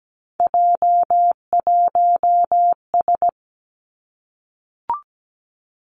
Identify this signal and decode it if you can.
Morse code, heard as a keyed tone, about 17 wpm, 700 Hz. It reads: J1S
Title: Three Challenging Letter-Number Combinations 17wpm